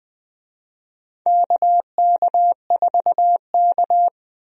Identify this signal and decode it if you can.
Morse code, heard as a keyed tone, about 20 wpm, 700 Hz. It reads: KK4K
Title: USA Call Signs - 4 Characters - 20wpm